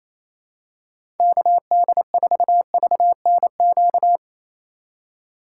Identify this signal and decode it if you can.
Morse code, heard as a keyed tone, about 28 wpm, 700 Hz. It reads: KD4VNQ
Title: USA Call Signs - 2x Character Spacing - 28wpm